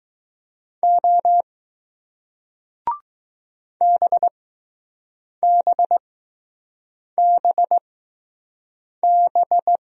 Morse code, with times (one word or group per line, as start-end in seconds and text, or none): 0.8-1.4 O
3.8-9.8 BBBB